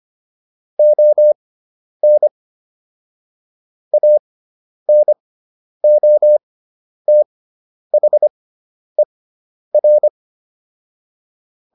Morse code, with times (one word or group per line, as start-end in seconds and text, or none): 0.8-2.3 ON
3.9-10.1 ANOTHER